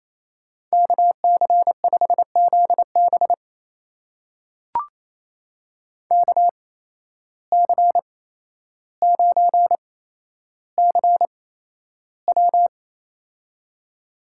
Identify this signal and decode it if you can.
Morse code, heard as a keyed tone, about 28 wpm, spaced as for 12 wpm, 700 Hz. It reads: KC5ZB KC9CW
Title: USA Call Signs - 5 Characters - 8x Character Spacing - 28wpm